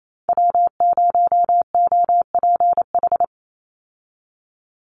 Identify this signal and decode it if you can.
Morse code, heard as a keyed tone, about 28 wpm, 700 Hz. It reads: W0OPH
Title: USA Call Signs 28wpm